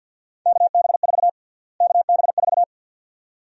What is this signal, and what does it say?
Morse code, heard as a keyed tone, about 49 wpm, 700 Hz. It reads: KB4 KB4
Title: USA Call Sign Prefixes - Repeated 2x - 50wpm